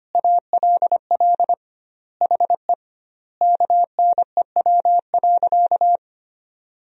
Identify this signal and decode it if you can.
Morse code, heard as a keyed tone, about 25 wpm, 700 Hz. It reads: ALL HE KNEW.
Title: 2x Word-Spacing - Sentences from Top 400 Words - 25wpm